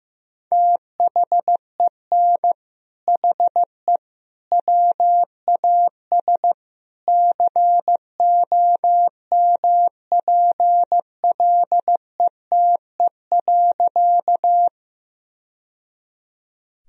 Morse code, heard as a keyed tone, about 15 wpm, 700 Hz. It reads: THEN HE WAS COMPLETE.
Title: Sentences from Top 1000 Words - Review Part 2 - 15wpm